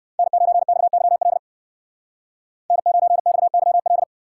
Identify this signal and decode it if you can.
Morse code, heard as a keyed tone, about 59 wpm, 700 Hz. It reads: N9XYL N9XYL